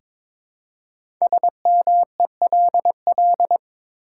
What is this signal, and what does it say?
Morse code, heard as a keyed tone, about 22 wpm, 700 Hz. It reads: SMELL
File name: 5 Letter Words - First 3 Letters Given - 22wpm